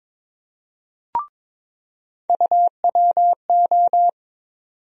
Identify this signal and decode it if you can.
Morse code, heard as a keyed tone, about 22 wpm, 700 Hz. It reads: UWO
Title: USA Call Sign Postfixes - 3 Letters 22wpm